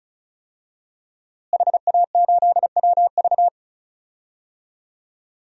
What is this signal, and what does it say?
Morse code, heard as a keyed tone, about 36 wpm, 700 Hz. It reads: HA8WV
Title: DXCC-100 Call Signs - 5 Characters - Repeated 2x - 35wpm